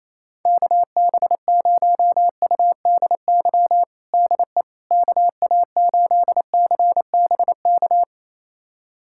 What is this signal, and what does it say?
Morse code, heard as a keyed tone, about 28 wpm, 700 Hz. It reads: KB0UDY DE KA8CBK
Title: USA Callsign DE Callsign - 6 Characters - 28wpm